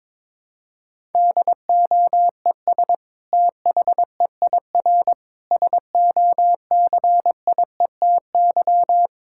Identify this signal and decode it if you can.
Morse code, heard as a keyed tone, about 22 wpm, 700 Hz. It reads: DOES THEIR SOCIETY
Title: Sentences from Top 1000 Words - Review Part 6 - 22wpm